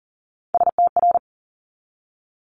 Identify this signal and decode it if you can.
Morse code, heard as a keyed tone, about 42 wpm, 700 Hz. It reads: STR
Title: Multi-letter Phonemes 40wpm